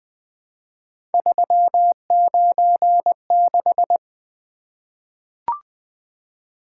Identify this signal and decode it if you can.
Morse code, heard as a keyed tone, about 20 wpm, 700 Hz. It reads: 396
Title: Three Letter-Number Combinations - 20wpm